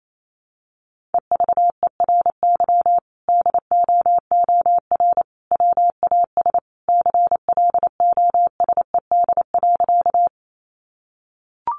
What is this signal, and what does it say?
Morse code, heard as a keyed tone, about 28 wpm, 700 Hz. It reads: EVERY DOOR WAS CLOSED.
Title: Sentences from Top 1400 Words 28wpm